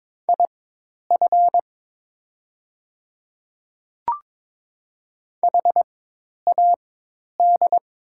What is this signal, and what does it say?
Morse code, heard as a keyed tone, about 22 wpm, spaced as for 10 wpm, 700 Hz. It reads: IF HAD
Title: 4x Character Spacing - Top 100 Words - 22wpm